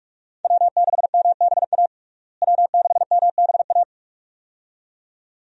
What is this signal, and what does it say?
Morse code, heard as a keyed tone, about 47 wpm, 700 Hz. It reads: W6MBA W6MBA
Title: USA Call Signs - 5 Characters - Repeated 2x - 45wpm